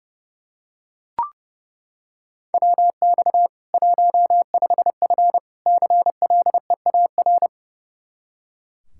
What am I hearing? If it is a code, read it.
Morse code, heard as a keyed tone, about 32 wpm, 700 Hz. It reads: WX 15F CLEAR
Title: Practice QSO - Exchange 2 - Info - 30wpm